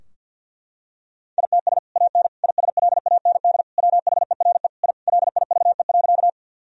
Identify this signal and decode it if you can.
Morse code, heard as a keyed tone, about 50 wpm, 700 Hz. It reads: ITS AN ISLAND WHERE I LIVE.
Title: Sentences from Top 1000 Words - Review Part 3 - 50wpm